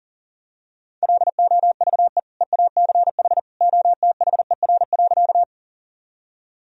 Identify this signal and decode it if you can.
Morse code, heard as a keyed tone, about 37 wpm, 700 Hz. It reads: LOVE EACH OTHER.
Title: Sentences from Top 1000 Words - Review Part 1 - 40wpm